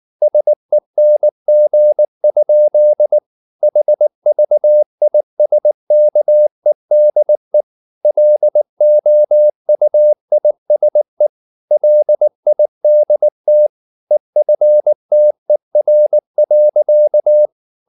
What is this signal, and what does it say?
Morse code, heard as a keyed tone, about 19 wpm, 600 Hz. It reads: SENG? HVISKEDE LOUISE LIDT EFTER.
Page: Ferie 19 wpm